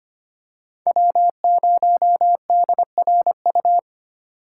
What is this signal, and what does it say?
Morse code, heard as a keyed tone, about 25 wpm, 700 Hz. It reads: W0DRU
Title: USA Call Signs - 4x Character Spacing - 25wpm